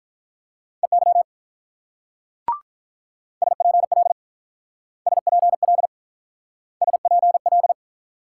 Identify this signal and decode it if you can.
Morse code, heard as a keyed tone, about 44 wpm, 700 Hz. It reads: EY SPL SPL SPL